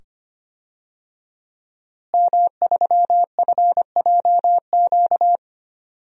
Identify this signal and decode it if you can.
Morse code, heard as a keyed tone, about 25 wpm, 700 Hz. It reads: M3FJQ
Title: DXCC-100 Call Signs - 5 Characters - Repeated 2x - 25wpm